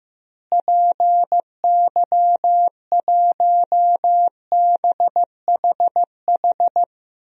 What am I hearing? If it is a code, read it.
Morse code, heard as a keyed tone, about 16 wpm, 700 Hz. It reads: PY1BHH
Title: DXCC-100 Call Signs - 6 Characters - Repeated 2x - 15wpm